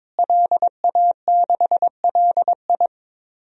Morse code, heard as a keyed tone, about 22 wpm, 700 Hz. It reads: LA6LI
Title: DXCC-100 Call Signs - 5 Characters - Repeated 2x - 22wpm